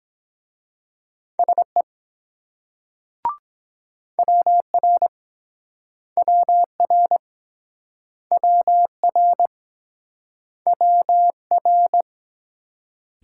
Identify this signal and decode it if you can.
Morse code, heard as a keyed tone, about 23 wpm, 700 Hz. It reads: SE WR WR WR WR